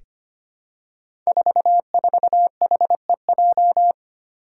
Morse code, heard as a keyed tone, about 25 wpm, 700 Hz. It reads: 44HEJ
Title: Five Challenging Letter-Number Combinations 25wpm